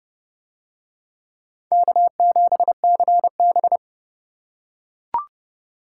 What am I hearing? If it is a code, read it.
Morse code, heard as a keyed tone, about 29 wpm, 700 Hz. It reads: K7CB